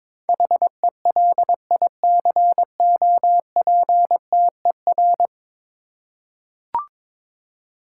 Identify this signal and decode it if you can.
Morse code, heard as a keyed tone, about 22 wpm, 700 Hz. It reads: HELICOPTER
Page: morse-code-ninja-practice-22wpm